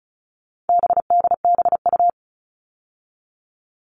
Morse code, heard as a keyed tone, about 34 wpm, 700 Hz. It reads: BDBU
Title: All Cut Numbers - 4-Digit 35wpm